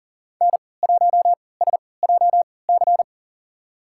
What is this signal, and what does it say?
Morse code, heard as a keyed tone, about 40 wpm, spaced as for 23 wpm, 700 Hz. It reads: N1SJC
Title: USA Call Signs - 5 Characters - 3x Character Spacing - 40wpm